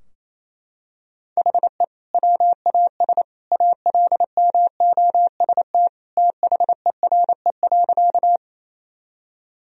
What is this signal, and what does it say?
Morse code, heard as a keyed tone, about 28 wpm, 700 Hz. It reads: HE WAS ALMOST THERE.
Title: Sentences from Top 1000 Words - Review Part 4 - 28wpm